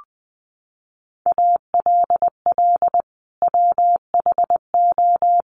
Morse code, heard as a keyed tone, about 20 wpm, 700 Hz. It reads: ALL WHO
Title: Sentences from Top 1400 Words 20wpm